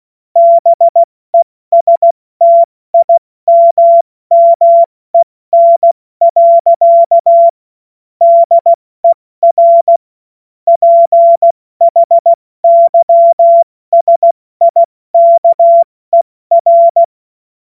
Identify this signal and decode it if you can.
Morse code, heard as a keyed tone, about 16 wpm, 680 Hz. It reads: BESTIMMEN. DER PHYSIKER